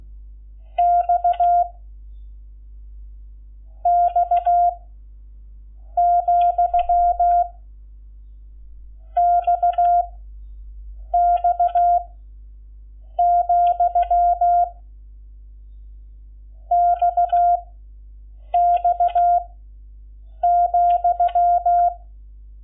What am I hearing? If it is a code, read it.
Morse code, heard as a keyed tone, about 16 wpm, 690 Hz. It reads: X X , X X , X X ,
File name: Lecciones / CLASE Nº11